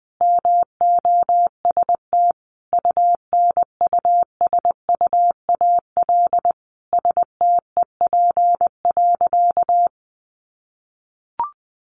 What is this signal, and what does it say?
Morse code, heard as a keyed tone, about 20 wpm, 700 Hz. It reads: MOST UNUSUAL STEP.
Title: Sentences from Top 1400 Words 20wpm